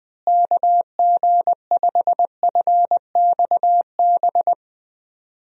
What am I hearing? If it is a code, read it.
Morse code, heard as a keyed tone, about 20 wpm, 700 Hz. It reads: KG5FXB
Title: USA Call Signs - 6 Characters - Repeated 3x - 20wpm